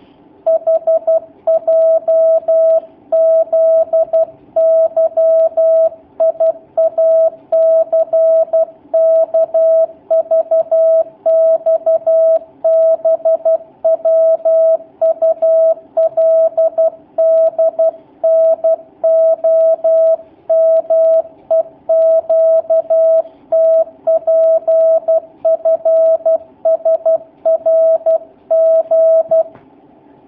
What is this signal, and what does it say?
Morse code, heard as a keyed tone, about 12 wpm, 650 Hz. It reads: HJZYIACKVXBWULDNOMEQTPFSRG
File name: jh8cbh / 2019cw / sp11- w